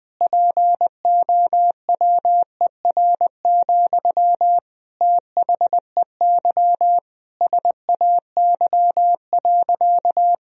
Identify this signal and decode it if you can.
Morse code, heard as a keyed tone, about 20 wpm, 700 Hz. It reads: POWER, THEY SAY.
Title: Sentences from Top 1000 Words - Review Part 2 - 20wpm